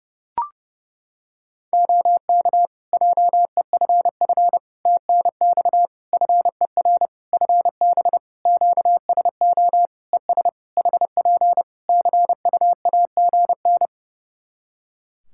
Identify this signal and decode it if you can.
Morse code, heard as a keyed tone, about 30 wpm, 700 Hz. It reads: OK JEFF TNX FER FB QSO ES HP CUAGN
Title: Practice QSO - Exchange 4 - Ending - 30wpm